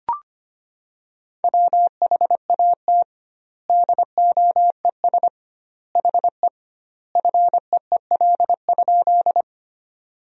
Morse code, heard as a keyed tone, about 25 wpm, 700 Hz. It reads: WHAT DOES HE FEEL?
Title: 2x Word-Spacing - Sentences from Top 400 Words - 25wpm